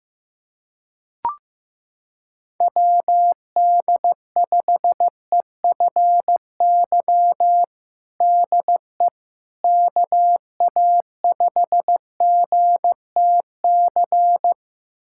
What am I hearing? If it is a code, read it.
Morse code, heard as a keyed tone, about 16 wpm, 700 Hz. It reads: WD5EFY DE KA5GTC
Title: USA Callsign DE Callsign - 6 Characters - 15wpm